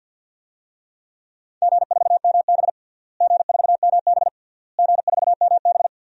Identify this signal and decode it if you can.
Morse code, heard as a keyed tone, about 51 wpm, 700 Hz. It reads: G4MB G4MB G4MB
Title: DXCC-100 Call Signs - 4 Characters - Repeated 3x - 50wpm